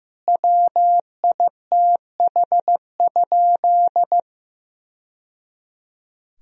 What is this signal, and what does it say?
Morse code, heard as a keyed tone, about 15 wpm, 700 Hz. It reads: WITH?
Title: Sentences from Top 1000 Words - Review Part 4 - 15wpm